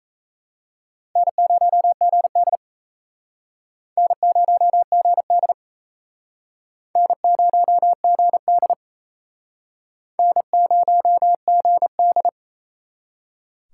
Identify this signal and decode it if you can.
Morse code, heard as a keyed tone, about 34 wpm, 700 Hz. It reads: N0GD N0GD N0GD N0GD
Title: Speed Racing - USA 1x2 Call Signs 28wpm